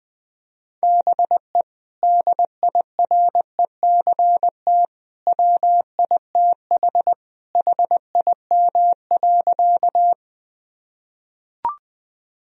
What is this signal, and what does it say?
Morse code, heard as a keyed tone, about 20 wpm, 700 Hz. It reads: BE DIRECT WITH HIM.
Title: Sentences from Top 1000 Words - Review Part 2 - 20wpm